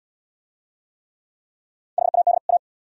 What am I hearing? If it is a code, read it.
Morse code, heard as a keyed tone, about 77 wpm, 690 Hz. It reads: HAD A